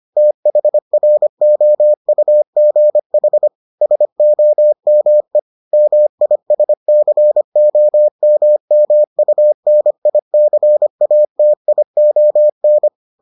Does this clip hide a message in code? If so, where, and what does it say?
Yes: Morse code from 0.2 to 12.9 s, heard as a keyed tone, about 25 wpm, 600 Hz. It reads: THROUGH SOME MISCOMMUNICATION